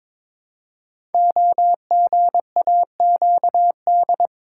Morse code, heard as a keyed tone, about 23 wpm, 700 Hz. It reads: OGAQD
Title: Code Groups - 5 Letters - 22wpm